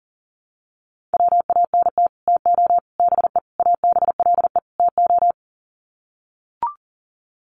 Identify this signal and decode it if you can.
Morse code, heard as a keyed tone, about 40 wpm, 700 Hz. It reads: WANT TO BE ABLE TO
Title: Sets of 5 Words 40wpm